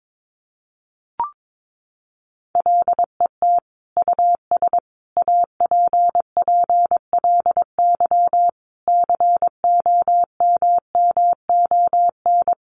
Morse code, heard as a keyed tone, about 22 wpm, 700 Hz. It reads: LET US APPLY COMMON
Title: Sentences from Top 1200 Words 22wpm